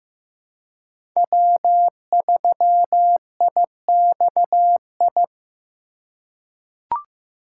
Morse code, heard as a keyed tone, about 15 wpm, 700 Hz. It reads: W3IXI
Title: USA Call Signs - 5 Characters - 2x Character Spacing - 15wpm